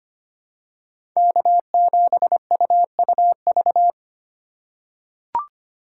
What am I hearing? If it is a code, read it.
Morse code, heard as a keyed tone, about 25 wpm, 700 Hz. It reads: K7UUV
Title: USA Call Signs - 5 Characters - 3x Character Spacing - 25wpm